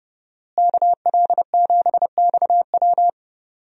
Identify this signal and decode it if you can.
Morse code, heard as a keyed tone, about 30 wpm, 700 Hz. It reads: KL7XW
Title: DXCC-100 Call Signs - 5 Characters - Repeated 2x - 30wpm